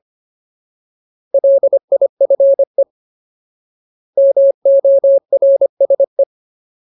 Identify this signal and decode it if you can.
Morse code, heard as a keyed tone, about 25 wpm, 550 Hz. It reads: LIFE MORSE
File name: Ultimate Word List / 25WPM Ultimate